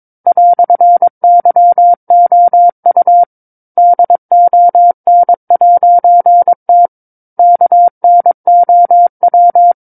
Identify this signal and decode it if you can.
Morse code, heard as a keyed tone, about 22 wpm, 700 Hz. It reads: "YOU DON'T KNOW
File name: 06-Chapter-06 22 WPM